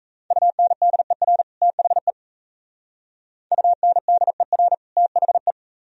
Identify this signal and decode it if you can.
Morse code, heard as a keyed tone, about 37 wpm, 700 Hz. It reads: UNDER THE UNDER THE